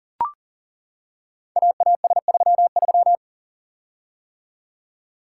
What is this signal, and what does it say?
Morse code, heard as a keyed tone, about 38 wpm, 700 Hz. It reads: AAS33
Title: Code Groups - 5 Letter:Numbers - 40wpm